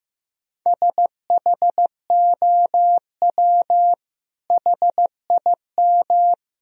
Morse code, heard as a keyed tone, about 15 wpm, 700 Hz. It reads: SHOW HIM
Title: Sentences from Top 1000 Words - Review Part 1 - 15wpm